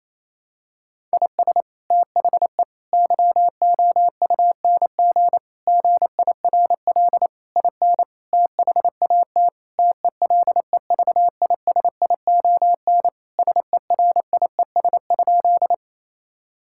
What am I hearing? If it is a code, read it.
Morse code, heard as a keyed tone, about 27 wpm, 700 Hz. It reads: IS THE YOUNG GIRL IN THAT TELEVISION SERIES?
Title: Sentences from Top 1000 Words - Review Part 5 - 28wpm